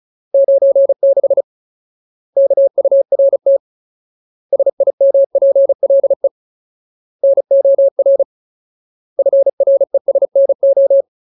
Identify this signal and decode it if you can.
Morse code, heard as a keyed tone, about 34 wpm, 550 Hz. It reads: 96 KURT SIMPLE NOR FRESNO
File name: Ultimate Word List / 35WPM Ultimate